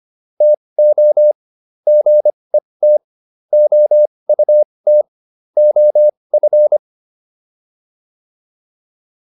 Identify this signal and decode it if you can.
Morse code, heard as a keyed tone, about 25 wpm, spaced as for 20 wpm, 600 Hz. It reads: TO GET OUT OF